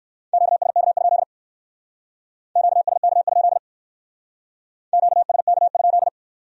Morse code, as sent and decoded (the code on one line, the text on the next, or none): --.- ... -.- ..--.. / --.- ... -.- ..--.. / --.- ... -.- ..--..
QSK? QSK? QSK?